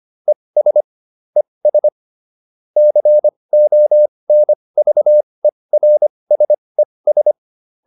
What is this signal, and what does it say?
Morse code, heard as a keyed tone, about 25 wpm, spaced as for 20 wpm, 600 Hz. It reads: ES ES CONVERSES
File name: ES8 20